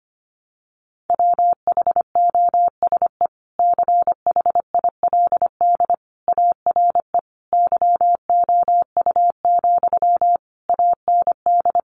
Morse code, heard as a keyed tone, about 25 wpm, 700 Hz. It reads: WHOSE CHILD ARE YOU, AND
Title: Sentences from Top 1200 Words 25wpm